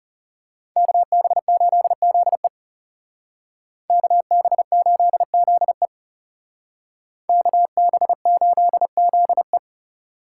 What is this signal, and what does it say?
Morse code, heard as a keyed tone, about 34 wpm, 700 Hz. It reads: KB8ZE KB8ZE KB8ZE
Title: Speed Racing - USA 2x2 Call Signs 30wpm